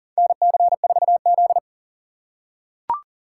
Morse code, heard as a keyed tone, about 40 wpm, 700 Hz. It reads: NC4Z
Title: USA Call Signs - 4 Characters - 2x Character Spacing - 40wpm